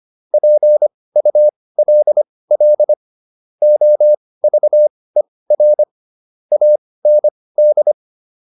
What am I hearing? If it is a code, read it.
Morse code, heard as a keyed tone, about 25 wpm, spaced as for 18 wpm, 600 Hz. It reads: PULL OVER AND